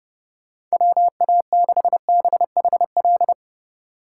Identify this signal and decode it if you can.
Morse code, heard as a keyed tone, about 39 wpm, 700 Hz. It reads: WA6BHL